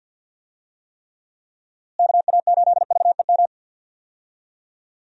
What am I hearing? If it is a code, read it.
Morse code, heard as a keyed tone, about 52 wpm, 700 Hz. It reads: KA8VEM
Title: USA Call Signs - 6 Characters - Repeated 2x - 50wpm